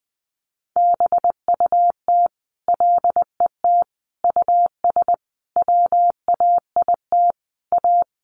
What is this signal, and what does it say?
Morse code, heard as a keyed tone, about 20 wpm, 700 Hz. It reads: BUT LET US WAIT A
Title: Sentences from Top 1500 Words 20wpm